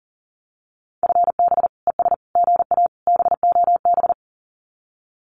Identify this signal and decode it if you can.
Morse code, heard as a keyed tone, about 41 wpm, 700 Hz. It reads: FB ES GA BOB
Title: Common QSO Phrases Encore 40wpm